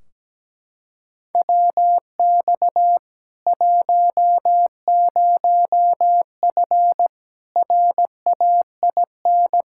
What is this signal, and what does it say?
Morse code, heard as a keyed tone, about 17 wpm, 700 Hz. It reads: WX 10F RAIN
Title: K1USN SST QSO Practice 17wpm